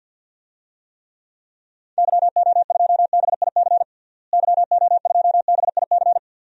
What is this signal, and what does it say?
Morse code, heard as a keyed tone, about 51 wpm, 700 Hz. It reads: YO2BIC YO2BIC
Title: DXCC-100 Call Signs - 6 Characters - Repeated 3x - 50wpm